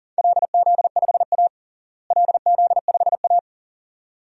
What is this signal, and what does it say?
Morse code, heard as a keyed tone, about 40 wpm, 700 Hz. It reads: LZ5A LZ5A
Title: DXCC-100 Call Signs - 4 Characters - Repeated 2x - 40wpm